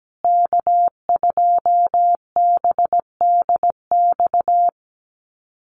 Morse code, heard as a keyed tone, about 17 wpm, 700 Hz. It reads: K2BDX
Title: USA Call Signs Repeated 2x 17wpm